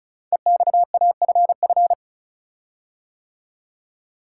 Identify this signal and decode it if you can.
Morse code, heard as a keyed tone, about 36 wpm, 700 Hz. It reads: EXAFF
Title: Code Groups - 5 Letters - 35wpm